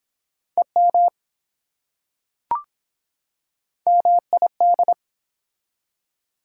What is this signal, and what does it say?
Morse code, heard as a keyed tone, about 25 wpm, 700 Hz. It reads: EM MID